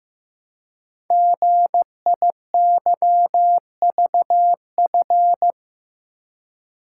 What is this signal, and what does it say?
Morse code, heard as a keyed tone, about 15 wpm, 700 Hz. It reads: GIYVF
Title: Code Groups - 5 Letters - 15wpm